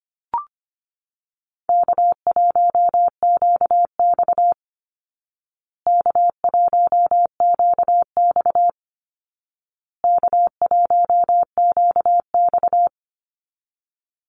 Morse code, heard as a keyed tone, about 25 wpm, 700 Hz. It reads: K1QX K1QX K1QX
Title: Easy Call Signs Repeated 3x - 25wpm